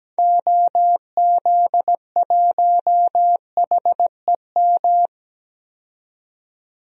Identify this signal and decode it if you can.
Morse code, heard as a keyed tone, about 17 wpm, 700 Hz. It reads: OZ1HEM
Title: DXCC-100 Call Signs - 6 Characters - 17wpm